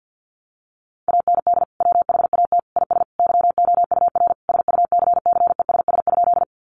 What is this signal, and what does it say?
Morse code, heard as a keyed tone, about 50 wpm, 700 Hz. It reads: AND WHAT IS YOUR SUCCESS?
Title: Sentences from Top 900 Words 50wpm